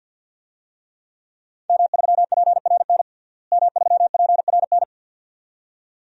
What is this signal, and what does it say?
Morse code, heard as a keyed tone, about 51 wpm, 700 Hz. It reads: M3PRN M3PRN
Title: DXCC-100 Call Signs - 5 Characters - Repeated 2x - 50wpm